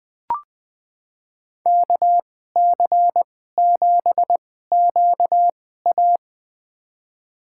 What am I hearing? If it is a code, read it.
Morse code, heard as a keyed tone, about 21 wpm, spaced as for 15 wpm, 700 Hz. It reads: KC7QA
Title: USA Call Signs - 5 Characters - 2x Character Spacing - 20wpm